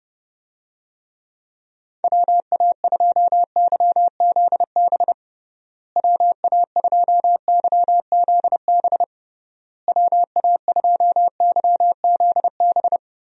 Morse code, heard as a keyed tone, about 32 wpm, 700 Hz. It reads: WA2YZB WA2YZB WA2YZB
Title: USA Call Signs - 6 Characters - Repeated 3x - 30wpm